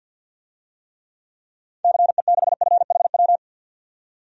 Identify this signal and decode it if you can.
Morse code, heard as a keyed tone, about 53 wpm, 700 Hz. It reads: CE6FHW
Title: DXCC-100 Call Signs - 6 Characters - 50wpm